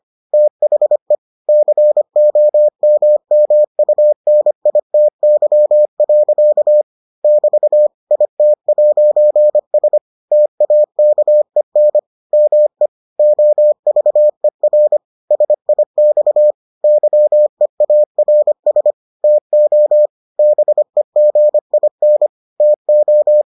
Morse code, heard as a keyed tone, about 25 wpm, 600 Hz. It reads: THE COMMUNITY. <BT> IT'S TAKEN ME OVER SIX YEARS TO BEGIN TO